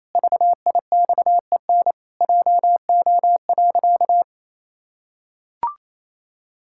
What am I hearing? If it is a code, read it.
Morse code, heard as a keyed tone, about 28 wpm, 700 Hz. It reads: VIXEN JO.
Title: Pangram Challenge v2 - 28wpm